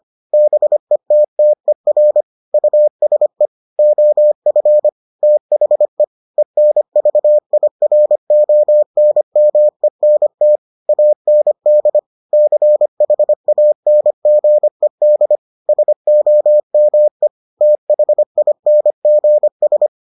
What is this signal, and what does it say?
Morse code, heard as a keyed tone, about 25 wpm, 600 Hz. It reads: BETTER USE OF THE ENVIRONMENT AND CHANGED SOME THINGS